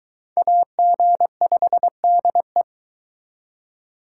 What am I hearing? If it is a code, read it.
Morse code, heard as a keyed tone, about 22 wpm, 700 Hz. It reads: AG5DE